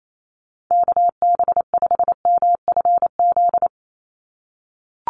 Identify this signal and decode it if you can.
Morse code, heard as a keyed tone, about 28 wpm, 700 Hz. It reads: KB5MFZ
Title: USA Call Signs 28wpm